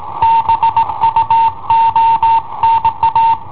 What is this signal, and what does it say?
Morse code, heard as a keyed tone, about 17 wpm, 910 Hz. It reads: BUONA
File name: i7ohp / cassetta i7ohp B